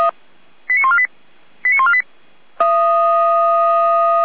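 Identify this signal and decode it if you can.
Five-tone selective call (ZVEI), heard as quick runs of five tones, tones of about 70 ms. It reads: 89148 89148
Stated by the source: Received by SDR.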